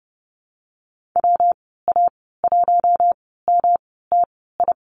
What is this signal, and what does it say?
Morse code, heard as a keyed tone, about 30 wpm, spaced as for 17 wpm, 700 Hz. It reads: WA1MTI
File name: Lesson 19g - Letter w - Call Signs 30wpm